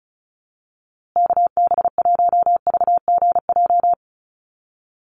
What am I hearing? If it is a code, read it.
Morse code, heard as a keyed tone, about 34 wpm, 700 Hz. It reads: KB1VGJ
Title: USA Call Signs Repeated 2x 35wpm